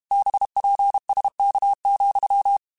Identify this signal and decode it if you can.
Morse code, heard as a keyed tone, about 32 wpm, 800 Hz. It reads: BPSK,
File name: QSOs / QSOs-mp3s / kc7sac@30